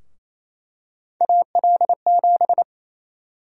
Morse code, heard as a keyed tone, about 27 wpm, 700 Hz. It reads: AL7
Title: USA Call Sign Prefixes - Repeated 2x - 28wpm